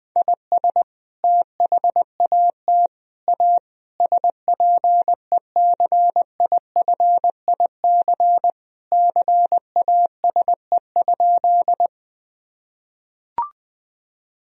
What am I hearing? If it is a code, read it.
Morse code, heard as a keyed tone, about 20 wpm, 700 Hz. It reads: IS THAT A SPECIFIC CASE?
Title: Sentences from Top 1000 Words - Review Part 3 - 20wpm